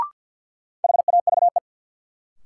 Morse code, heard as a keyed tone, about 48 wpm, 700 Hz. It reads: HAVE
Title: Sentences from Top 100 Words - New Word Drill - 50wpm